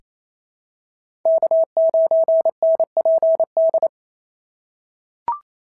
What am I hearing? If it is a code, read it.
Morse code, heard as a keyed tone, about 27 wpm, 690 Hz. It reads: K9NPD
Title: SOTA QSO Practice 28wpm